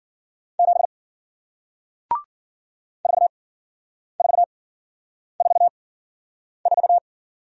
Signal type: Morse code

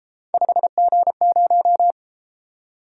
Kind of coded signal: Morse code

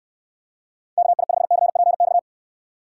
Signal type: Morse code

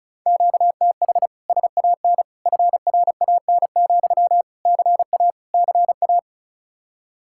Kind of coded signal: Morse code

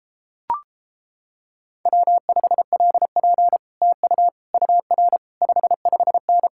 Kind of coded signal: Morse code

